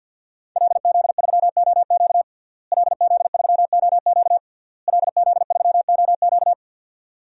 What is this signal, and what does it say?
Morse code, heard as a keyed tone, about 53 wpm, 700 Hz. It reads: LZ3OQ LZ3OQ LZ3OQ
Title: DXCC-100 Call Signs - 5 Characters - Repeated 3x - 50wpm